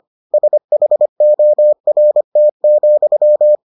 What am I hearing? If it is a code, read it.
Morse code, heard as a keyed tone, about 25 wpm, 600 Hz. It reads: SHORT,